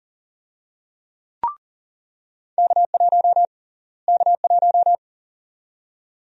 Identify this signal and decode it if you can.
Morse code, heard as a keyed tone, about 38 wpm, 700 Hz. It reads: K1 K1
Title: USA Call Sign Prefixes - Repeated 2x - 40wpm